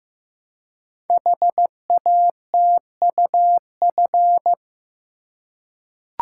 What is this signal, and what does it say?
Morse code, heard as a keyed tone, about 15 wpm, 700 Hz. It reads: HATUF
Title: Code Groups - 5 Letters - 15wpm